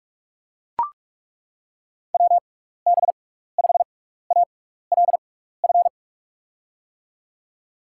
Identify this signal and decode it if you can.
Morse code, heard as a keyed tone, about 41 wpm, spaced as for 15 wpm, 700 Hz. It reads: WB5ALF
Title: USA Call Signs - 6 Characters - 6x Character Spacing - 45wpm